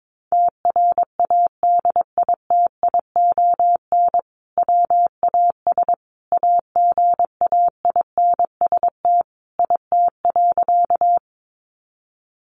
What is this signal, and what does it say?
Morse code, heard as a keyed tone, about 22 wpm, 700 Hz. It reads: TRADITION WAS AGAINST IT.
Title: Sentences from Top 1500 Words 22wpm